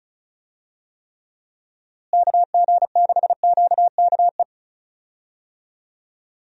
Morse code, heard as a keyed tone, about 34 wpm, 700 Hz. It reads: KG6QKE
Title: USA Call Signs - 6 Characters - Repeated 2x - 35wpm